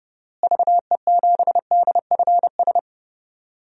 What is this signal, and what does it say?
Morse code, heard as a keyed tone, about 30 wpm, 700 Hz. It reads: VE7DFS
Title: DXCC-100 Call Signs - 6 Characters - Repeated 3x - 30wpm